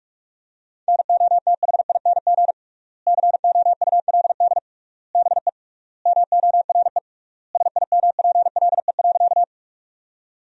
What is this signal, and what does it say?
Morse code, heard as a keyed tone, about 46 wpm, 700 Hz. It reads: NOTHING COULD BE MORE SIMPLE.
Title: 3x Word-Spacing - Sentences from Top 500 Words - 45wpm